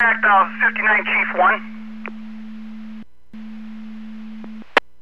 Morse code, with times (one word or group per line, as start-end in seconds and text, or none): none